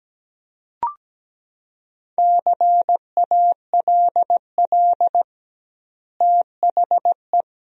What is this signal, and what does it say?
Morse code, heard as a keyed tone, about 17 wpm, 700 Hz. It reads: CALL THE
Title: 2x Word-Spacing - Sentences from Top 300 Words - 17wpm